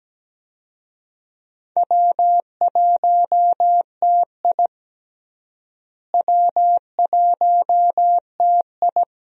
Morse code, heard as a keyed tone, about 17 wpm, 700 Hz. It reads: W1TI W1TI
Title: USA Call Signs - 4 Characters - Repeated 2x - 17wpm